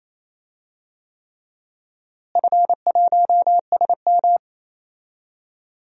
Morse code, heard as a keyed tone, about 28 wpm, 700 Hz. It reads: F1SM
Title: DXCC-100 Call Signs - 4 Characters - Repeated 3x - 28wpm